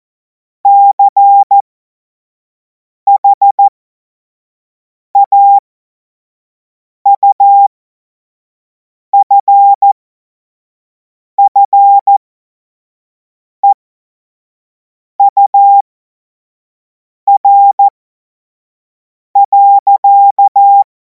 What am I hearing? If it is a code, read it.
Morse code, heard as a keyed tone, about 14 wpm, spaced as for 5 wpm, 800 Hz. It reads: CHAUFFEUR.
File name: Morsekurs / Witze / Witz 24. März 2026 25 F70